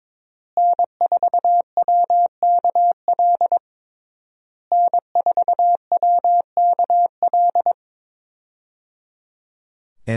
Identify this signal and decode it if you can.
Morse code, heard as a keyed tone, about 22 wpm, 700 Hz. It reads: N4WKL N4WKL
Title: USA Call Signs - 5 Characters - Repeated 3x - 22wpm